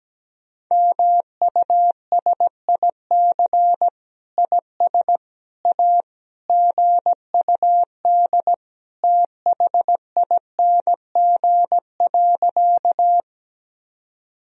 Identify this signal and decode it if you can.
Morse code, heard as a keyed tone, about 17 wpm, 700 Hz. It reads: MUSIC IS A GUD THING.
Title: Sentences from Top 300 Words and Abbreviations 17wpm